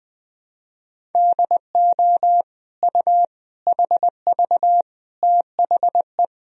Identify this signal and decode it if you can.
Morse code, heard as a keyed tone, about 20 wpm, 700 Hz. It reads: DO U HV THE
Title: Sentences from Top 300 Words and Abbreviations 20wpm